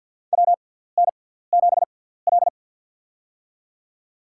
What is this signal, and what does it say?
Morse code, heard as a keyed tone, about 47 wpm, spaced as for 17 wpm, 700 Hz. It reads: WN7L
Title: USA Call Signs - 6x Character Spacing - 50wpm